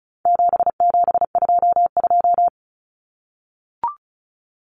Morse code, heard as a keyed tone, about 34 wpm, 700 Hz. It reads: 7722
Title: Numbers - 4-Digit 35wpm